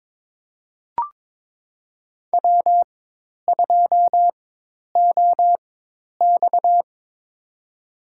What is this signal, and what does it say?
Morse code, heard as a keyed tone, about 22 wpm, spaced as for 10 wpm, 700 Hz. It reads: W2OX
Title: USA Call Signs - 4 Characters - 4x Character Spacing - 22wpm